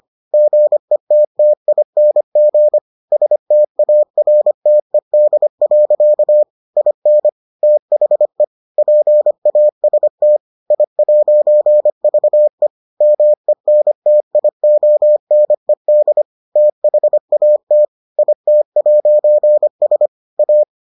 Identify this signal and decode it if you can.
Morse code, heard as a keyed tone, about 25 wpm, 600 Hz. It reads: GETTING STARTED. IN THE PAST I'VE MENTIONED THAT IT'S A